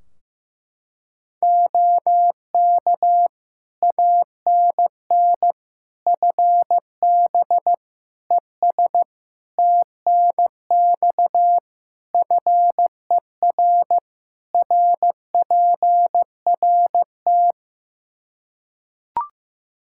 Morse code, heard as a keyed tone, about 16 wpm, 700 Hz. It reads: OK ANN FB ES TNX FER RPRT
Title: K1USN SST QSO Practice 15wpm